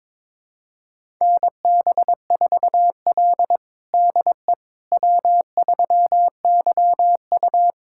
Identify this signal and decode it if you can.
Morse code, heard as a keyed tone, about 22 wpm, 700 Hz. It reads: NB4L DE W3YU
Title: USA Callsign DE Callsign - 4 Characters - 22wpm